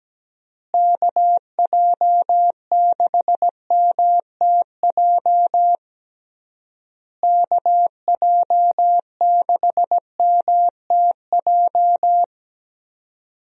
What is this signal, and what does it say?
Morse code, heard as a keyed tone, about 17 wpm, 700 Hz. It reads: KJ6MTJ KJ6MTJ
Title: USA Call Signs - 6 Characters - Repeated 3x - 17wpm